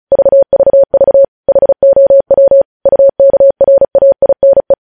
Morse code, heard as a keyed tone, about 36 wpm, 550 Hz. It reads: VVV HOW UKRAINE
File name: podcast / News headlines-1x-35x35